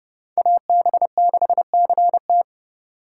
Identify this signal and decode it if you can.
Morse code, heard as a keyed tone, about 29 wpm, 700 Hz. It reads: AB6CT